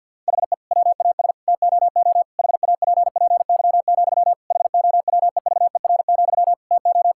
Morse code, heard as a keyed tone, about 50 wpm, 700 Hz. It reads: HE WAS TOO HAPPY, HOWEVER, TO
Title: Sentences from Top 1000 Words - Review Part 6 - 50wpm